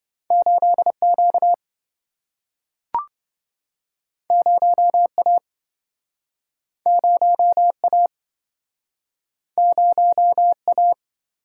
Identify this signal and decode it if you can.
Morse code, heard as a keyed tone, about 26 wpm, 700 Hz. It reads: 8Q 0A 0A 0A